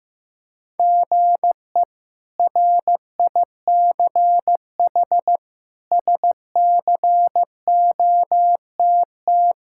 Morse code, heard as a keyed tone, about 15 wpm, 700 Hz. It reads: GE RICH SCOTT
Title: K1USN SST QSO Practice - 15wpm